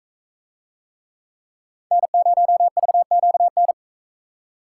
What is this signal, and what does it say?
Morse code, heard as a keyed tone, about 42 wpm, 700 Hz. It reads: N0VQN